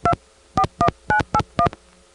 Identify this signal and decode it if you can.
DTMF, heard as telephone touch tones, tones of about 80 ms. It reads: 241981